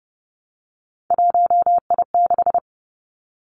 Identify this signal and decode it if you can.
Morse code, heard as a keyed tone, about 30 wpm, 700 Hz. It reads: 1I6
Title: Three Challenging Letter-Number Combinations 30wpm